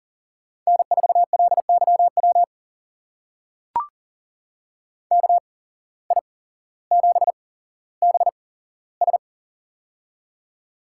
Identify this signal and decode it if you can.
Morse code, heard as a keyed tone, about 42 wpm, spaced as for 17 wpm, 700 Hz. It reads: N4LYW KI7BS
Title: USA Call Signs - 8x Character Spacing - 40wpm